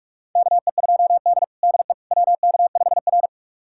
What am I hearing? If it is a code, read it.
Morse code, heard as a keyed tone, about 48 wpm, 700 Hz. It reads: KE2D DE WK5R
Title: USA Callsign DE Callsign - 4 Characters - 45wpm